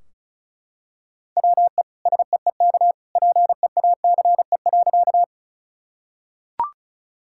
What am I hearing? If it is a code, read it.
Morse code, heard as a keyed tone, about 36 wpm, 700 Hz. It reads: WE SEEK PEACE.
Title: Sentences from Top 1600 Words 35wpm